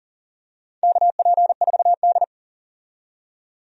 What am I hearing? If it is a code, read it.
Morse code, heard as a keyed tone, about 38 wpm, 700 Hz. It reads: KP4D
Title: DXCC-100 Call Signs - 4 Characters - 40wpm